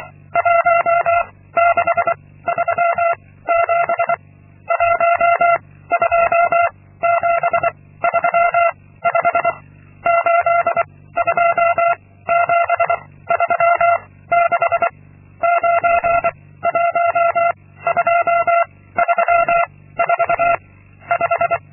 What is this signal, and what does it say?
Morse code, heard as a keyed tone, about 25 wpm, 680 Hz. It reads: 1 6 3 7 1 2 7 3 5 8 2 7 3 6 9 1 2 3 4 5
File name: Linksmakalnis-2